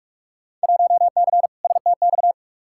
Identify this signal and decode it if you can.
Morse code, heard as a keyed tone, about 47 wpm, 700 Hz. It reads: 1C STX
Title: Field Day Exchanges Encore - 45wpm